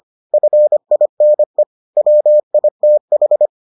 Morse code, heard as a keyed tone, about 25 wpm, 600 Hz. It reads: FINE WITH